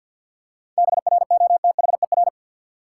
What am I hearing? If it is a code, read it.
Morse code, heard as a keyed tone, about 47 wpm, 700 Hz. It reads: BROTHER
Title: Seven Letter Words - 50wpm